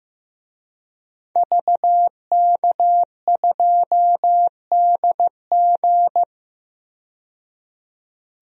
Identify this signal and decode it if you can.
Morse code, heard as a keyed tone, about 15 wpm, 700 Hz. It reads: VK2DG
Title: DXCC-100 Call Signs - 5 Characters - 15wpm